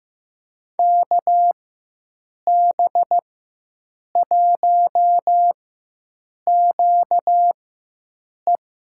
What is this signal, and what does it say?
Morse code, heard as a keyed tone, about 15 wpm, spaced as for 7 wpm, 700 Hz. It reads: KB1QE
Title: USA Call Signs - 4x Character Spacing - 15wpm